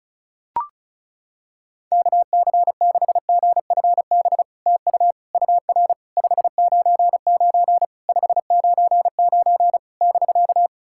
Morse code, heard as a keyed tone, about 36 wpm, 700 Hz. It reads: KC6GFB TU UR 599 599 <BK>
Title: SOTA QSO Practice 35wpm